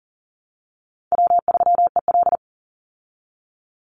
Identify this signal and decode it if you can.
Morse code, heard as a keyed tone, about 40 wpm, 700 Hz. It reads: W3EL
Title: Easy Call Signs Repeated 2x - 40wpm